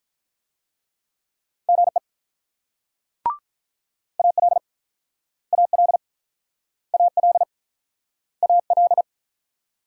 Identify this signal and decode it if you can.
Morse code, heard as a keyed tone, about 48 wpm, 700 Hz. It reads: GE AL AL AL AL